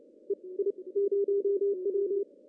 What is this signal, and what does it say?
Morse code, heard as a keyed tone, about 29 wpm, spaced as for 23 wpm, 400 Hz. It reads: EI0W
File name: CW / cqww-cw-k5tr